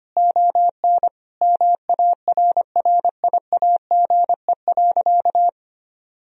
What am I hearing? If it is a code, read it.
Morse code, heard as a keyed tone, about 25 wpm, 700 Hz. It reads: ON MARRIAGE.
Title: Sentences from Top 1000 Words - Review Part 1 - 25wpm